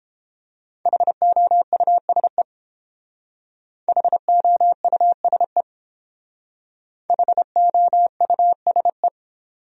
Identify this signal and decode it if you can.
Morse code, heard as a keyed tone, about 30 wpm, 700 Hz. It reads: HOUSE HOUSE HOUSE